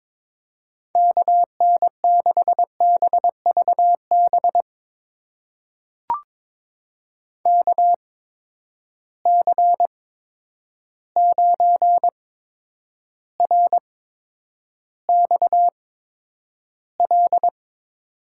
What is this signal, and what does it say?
Morse code, heard as a keyed tone, about 23 wpm, spaced as for 10 wpm, 700 Hz. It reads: KN6BVB KC9RXL
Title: USA Call Signs - 6 Characters - 8x Character Spacing - 22wpm